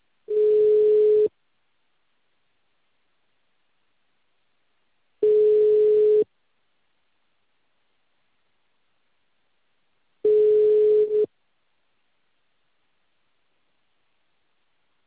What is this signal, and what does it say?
Ringback tone (Europe, 425 Hz).